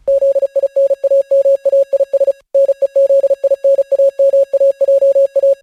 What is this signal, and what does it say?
Morse code, heard as a keyed tone, about 36 wpm, 560 Hz. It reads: ZINAMAIS NEZINAMAJA